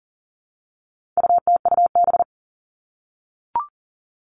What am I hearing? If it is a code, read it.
Morse code, heard as a keyed tone, about 40 wpm, 700 Hz. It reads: UTUB
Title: All Cut Numbers - 4-Digit 40wpm